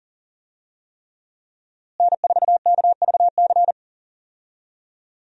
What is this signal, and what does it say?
Morse code, heard as a keyed tone, about 37 wpm, 700 Hz. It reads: N4KVC
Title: USA Call Signs - 5 Characters - Repeated 3x - 40wpm